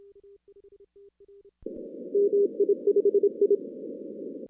Morse code, heard as a keyed tone, about 25 wpm, 400 Hz. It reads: MI5I
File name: 2016 arrl dx cw / 2016-ARRL-DX-CW-K5TR- - 0-46